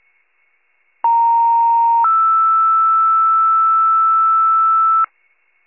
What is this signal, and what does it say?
Two-tone page, heard as a one-second tone, then a three-second tone, about 930 then 1320 Hz, with noise. It is heard from 1.0 to 5.1 s.
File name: wcalert